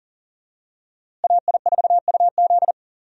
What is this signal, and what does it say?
Morse code, heard as a keyed tone, about 39 wpm, 700 Hz. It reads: AI4UZ
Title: USA Call Signs - 5 Characters - Repeated 3x - 40wpm